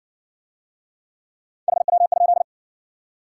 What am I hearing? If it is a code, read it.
Morse code, heard as a keyed tone, about 55 wpm, 700 Hz. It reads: HW?